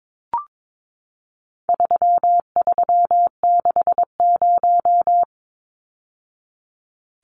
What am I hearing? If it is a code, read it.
Morse code, heard as a keyed tone, about 22 wpm, 700 Hz. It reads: 3360
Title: Numbers - 4-Digit 22wpm